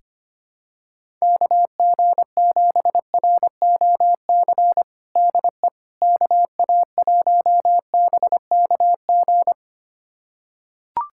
Morse code, heard as a keyed tone, about 25 wpm, 700 Hz. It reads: KG7ROC DE KA1BKG
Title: USA Callsign DE Callsign - 6 Characters - 25wpm